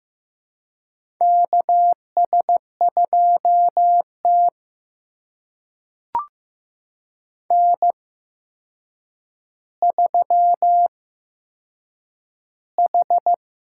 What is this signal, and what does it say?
Morse code, heard as a keyed tone, about 15 wpm, spaced as for 7 wpm, 700 Hz. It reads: KS2T N3H
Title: USA Call Signs - 8x Character Spacing - 15wpm